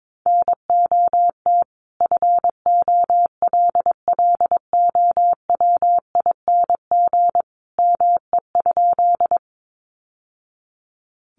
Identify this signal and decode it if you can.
Morse code, heard as a keyed tone, about 22 wpm, 700 Hz. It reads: NOT FOLLOWING ME?
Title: Sentences from Top 700 Words 22wpm